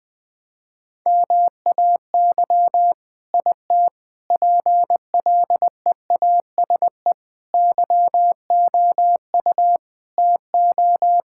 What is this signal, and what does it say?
Morse code, heard as a keyed tone, about 20 wpm, 700 Hz. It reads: MAY IT PLEASE YOU TO
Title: Sentences from Top 1600 Words 20wpm